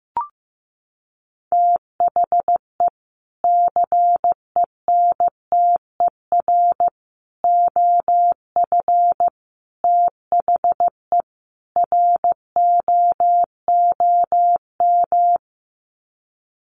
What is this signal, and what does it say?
Morse code, heard as a keyed tone, about 15 wpm, 700 Hz. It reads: THE CENTER OF THE ROOM
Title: Sets of 5 Words Encore 15wpm